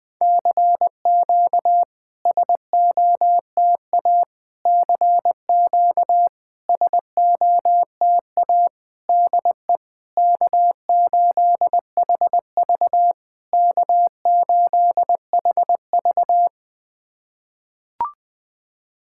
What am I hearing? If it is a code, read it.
Morse code, heard as a keyed tone, about 21 wpm, 690 Hz. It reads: CQ SOTA CQ SOTA DE K8HV K8HV
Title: SOTA QSO Practice 20wpm